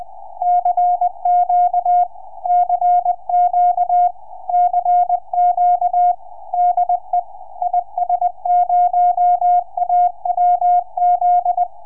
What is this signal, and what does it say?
Morse code, heard as a keyed tone, about 20 wpm, 710 Hz. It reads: CQ CQ CQ DE IS0AWZ
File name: is0awz 4m